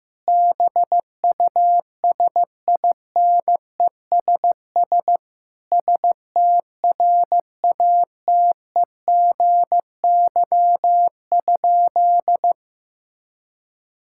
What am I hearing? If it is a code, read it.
Morse code, heard as a keyed tone, about 16 wpm, 700 Hz. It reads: BUSINESS STRATEGY?
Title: Sentences from Top 1000 Words - Review Part 5 - 15wpm